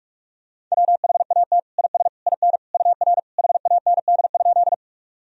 Morse code, heard as a keyed tone, about 46 wpm, 700 Hz. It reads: WHAT IS IN UR HAND?
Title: Sentences from Top 200 Words and Abbreviations 45wpm